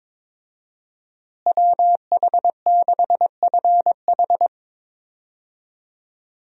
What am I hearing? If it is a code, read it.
Morse code, heard as a keyed tone, about 22 wpm, 700 Hz. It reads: WH6FH
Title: USA Call Signs - 5 Characters - Repeated 3x - 22wpm